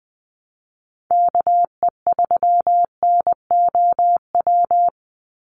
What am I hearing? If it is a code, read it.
Morse code, heard as a keyed tone, about 20 wpm, 700 Hz. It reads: KE3NOW
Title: USA Call Signs 20wpm